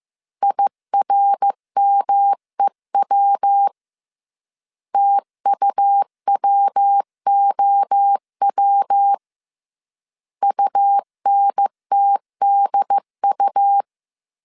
Morse code, heard as a keyed tone, about 15 wpm, 800 Hz. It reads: IRMEW TUWOW UNTDU
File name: CW / DET / 1 3 1500C S C 14wpm 2458